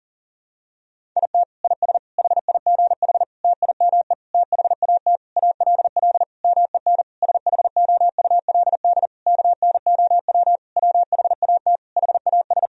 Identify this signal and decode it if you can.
Morse code, heard as a keyed tone, about 40 wpm, 700 Hz. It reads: IT IS HIGH TIME THAT ALL MEN SHOULD KNOW WHAT HAS
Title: Sentences from Top 1000 Words - Review Part 1 - 40wpm